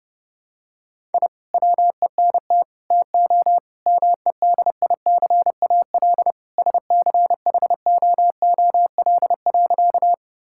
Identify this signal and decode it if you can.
Morse code, heard as a keyed tone, about 32 wpm, 700 Hz. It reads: I WENT TO MEDICAL SCHOOL.
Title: Sentences from Top 1000 Words - Review Part 5 - 30wpm